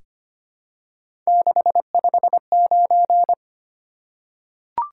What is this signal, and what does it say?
Morse code, heard as a keyed tone, about 25 wpm, 700 Hz. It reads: 659
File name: Three Letter-Number Combinations - 25wpm